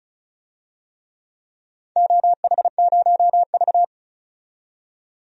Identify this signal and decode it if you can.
Morse code, heard as a keyed tone, about 36 wpm, 700 Hz. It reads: OH0V
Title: DXCC-100 Call Signs - 4 Characters - 35wpm